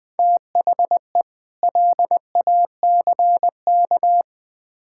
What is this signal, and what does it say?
Morse code, heard as a keyed tone, about 20 wpm, 700 Hz. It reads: THE LACK
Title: Build A Phrase - 3-Word Phrases - 20wpm